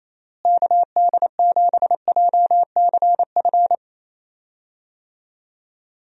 Morse code, heard as a keyed tone, about 28 wpm, 700 Hz. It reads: KD7JCF
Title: USA Call Signs - 6 Characters - Repeated 2x - 28wpm